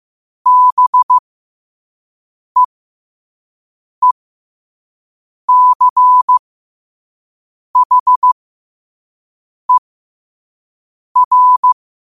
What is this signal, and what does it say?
Morse code, heard as a keyed tone, about 15 wpm, spaced as for 5 wpm, 1000 Hz. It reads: BEECHER
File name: qotdfiles / 5wpm